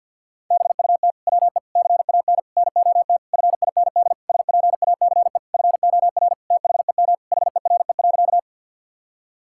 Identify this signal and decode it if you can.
Morse code, heard as a keyed tone, about 50 wpm, 700 Hz. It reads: BUT WE CAN NOT FIND SPACE FOR THEM HERE.
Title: Sentences from Top 1000 Words - Review Part 5 - 50wpm